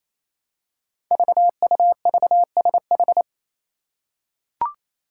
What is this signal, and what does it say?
Morse code, heard as a keyed tone, about 27 wpm, 700 Hz. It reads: VUVSH
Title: Five Challenging Letter-Number Combinations 28wpm